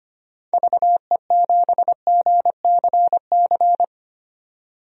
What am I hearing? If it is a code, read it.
Morse code, heard as a keyed tone, about 25 wpm, 700 Hz. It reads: VE7GCC
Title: DXCC-100 Call Signs - 6 Characters - 25wpm